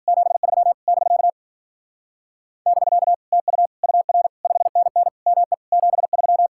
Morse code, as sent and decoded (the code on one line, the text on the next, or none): --... ...-- / -...-.- / -...-.- / - ..- / ..- .-. / ..... -. -. / -- . / --... ...--
73 <BK> <BK> TU UR 5NN ME 73